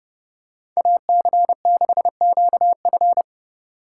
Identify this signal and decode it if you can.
Morse code, heard as a keyed tone, about 29 wpm, 700 Hz. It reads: AC6QF